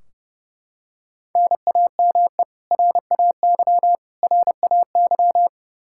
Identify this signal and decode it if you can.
Morse code, heard as a keyed tone, about 32 wpm, 700 Hz. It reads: NAME RAY RAY
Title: Practice QSO - Exchange 1 - Beginning - 30wpm